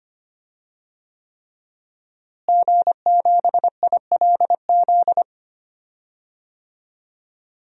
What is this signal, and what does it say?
Morse code, heard as a keyed tone, about 25 wpm, 700 Hz. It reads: G7ILZ
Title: DXCC-100 Call Signs - 5 Characters - Repeated 3x - 25wpm